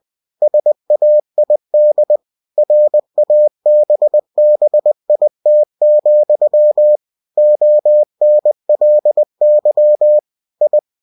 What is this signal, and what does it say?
Morse code, heard as a keyed tone, about 20 wpm, 600 Hz. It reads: SAID RABBIT, ONLY I
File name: cw-books / winnie-the-pooh / winnie-20wpm-page-0017